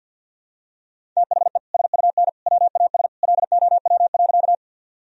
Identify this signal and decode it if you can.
Morse code, heard as a keyed tone, about 47 wpm, 700 Hz. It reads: THE SUN WAS LOW.
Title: Sentences from Top 1000 Words - Review Part 1 - 50wpm